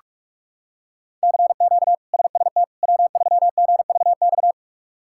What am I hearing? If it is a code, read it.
Morse code, heard as a keyed tone, about 46 wpm, 700 Hz. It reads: CQ SST W3GVX
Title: K1USN SST QSO Practice - 45wpm